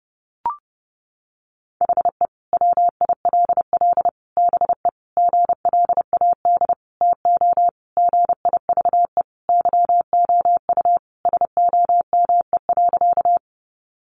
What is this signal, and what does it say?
Morse code, heard as a keyed tone, about 30 wpm, 700 Hz. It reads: HE WILL BE GLAD TO GIVE YOU SOME.
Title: Sentences from Top 800 Words 30wpm